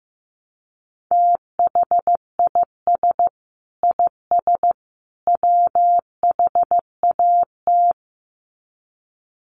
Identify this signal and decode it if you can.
Morse code, heard as a keyed tone, about 15 wpm, 700 Hz. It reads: THIS IS WHAT
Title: Sets of 3 Words 15wpm